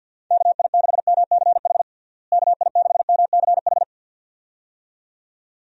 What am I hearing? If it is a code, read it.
Morse code, heard as a keyed tone, about 52 wpm, 700 Hz. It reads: KI6MCH KI6MCH
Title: USA Call Signs - 6 Characters - Repeated 2x - 50wpm